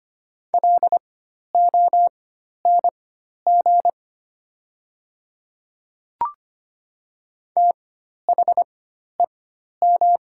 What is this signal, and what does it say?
Morse code, heard as a keyed tone, about 25 wpm, spaced as for 12 wpm, 700 Hz. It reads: LONG THEM
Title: 4x Character Spacing - Top 200 Words - 25wpm